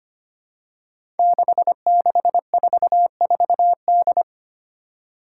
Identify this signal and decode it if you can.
Morse code, heard as a keyed tone, about 25 wpm, 700 Hz. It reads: 6644D
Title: Five Challenging Letter-Number Combinations 25wpm